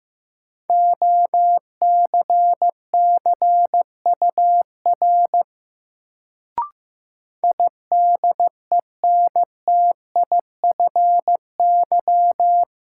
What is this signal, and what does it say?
Morse code, heard as a keyed tone, about 15 wpm, 700 Hz. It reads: OCCUR IDENTIFY
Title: Sentences from Top 900 Words - New Word Drill - 15wpm